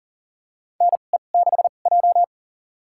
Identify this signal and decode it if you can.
Morse code, heard as a keyed tone, about 39 wpm, spaced as for 29 wpm, 700 Hz. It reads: NE6J
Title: USA Call Signs - 4 Characters - 2x Character Spacing - 40wpm